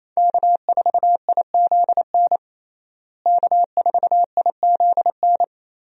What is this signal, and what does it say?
Morse code, heard as a keyed tone, about 28 wpm, 700 Hz. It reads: K4IZN K4IZN
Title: USA Call Signs - 5 Characters - Repeated 2x - 28wpm